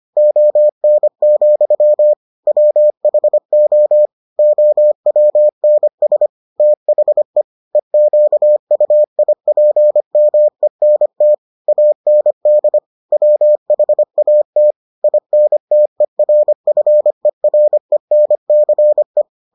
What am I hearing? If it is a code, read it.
Morse code, heard as a keyed tone, about 25 wpm, 600 Hz. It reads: ON, WHO OWNS THE EQUIPMENT AND WHAT INTERFERENCE